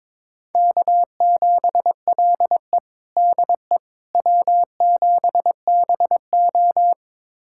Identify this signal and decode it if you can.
Morse code, heard as a keyed tone, about 22 wpm, 700 Hz. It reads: K7LE DE W7BO
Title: USA Callsign DE Callsign - 4 Characters - 22wpm